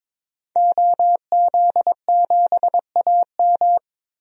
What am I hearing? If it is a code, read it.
Morse code, heard as a keyed tone, about 22 wpm, 700 Hz. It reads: OZ7AM
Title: DXCC-100 Call Signs - 5 Characters - 22wpm